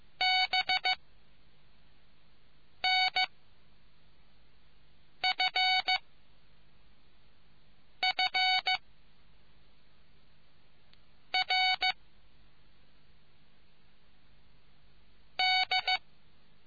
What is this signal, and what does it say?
Morse code, heard as a keyed tone, about 15 wpm, 750 Hz. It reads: B N F F R D